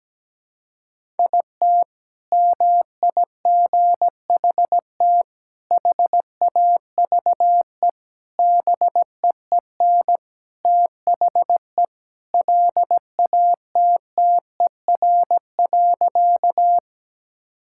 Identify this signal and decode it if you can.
Morse code, heard as a keyed tone, about 17 wpm, 700 Hz. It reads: IT MIGHT HAVE BEEN THE LATTER.
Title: Sentences from Top 1600 Words 17wpm